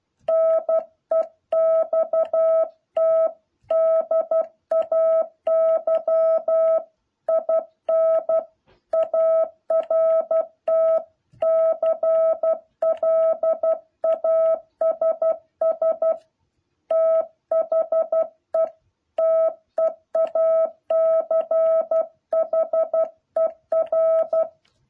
Morse code, heard as a keyed tone, about 12 wpm, 650 Hz. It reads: NEXT DAY IN ART CLASS THE TEACHER
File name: jh8cbh / 2021cw